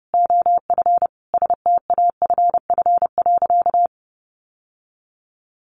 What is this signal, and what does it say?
Morse code, heard as a keyed tone, about 30 wpm, 700 Hz. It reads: OF STAFF.
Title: Sentences from Top 1200 Words 30wpm